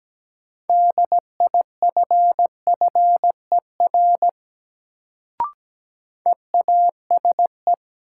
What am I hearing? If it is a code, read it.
Morse code, heard as a keyed tone, about 17 wpm, 700 Hz. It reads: DIFFER EASE
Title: Sentences from Top 1000 Words - Review Part 1 - New Word Drill - 17wpm